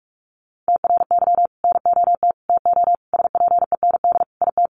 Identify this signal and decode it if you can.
Morse code, heard as a keyed tone, about 44 wpm, 700 Hz. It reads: TRY NOT TO SPEND IT
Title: Sentences from Top 800 Words 45wpm